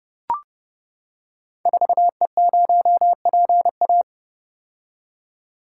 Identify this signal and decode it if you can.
Morse code, heard as a keyed tone, about 32 wpm, 700 Hz. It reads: 4E0PA
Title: Code Groups - 5 Letter:Numbers - 30wpm